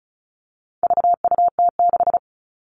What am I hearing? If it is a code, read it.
Morse code, heard as a keyed tone, about 34 wpm, 700 Hz. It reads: VUT6
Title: All Cut Numbers - 4-Digit 35wpm